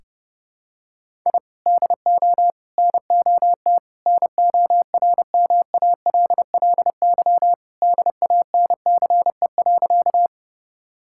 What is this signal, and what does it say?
Morse code, heard as a keyed tone, about 30 wpm, 700 Hz. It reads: I DO NOT NORMALLY DANCE.
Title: Sentences from Top 1000 Words - Review Part 6 - 30wpm